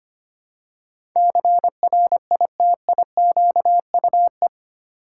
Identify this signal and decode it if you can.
Morse code, heard as a keyed tone, about 25 wpm, 700 Hz. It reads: CRITIQUE
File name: 8 Letter Words - First 3 Letters Given - 25wpm